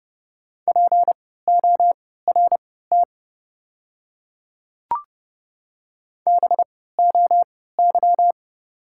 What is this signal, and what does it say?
Morse code, heard as a keyed tone, about 30 wpm, spaced as for 17 wpm, 700 Hz. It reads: PORT BOY
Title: 3x Character Spacing - Top 300 Words - 30wpm